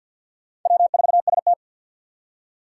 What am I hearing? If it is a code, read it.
Morse code, heard as a keyed tone, about 51 wpm, 700 Hz. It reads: W4ST
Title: Four Challenging Letter-Number Combinations 50wpm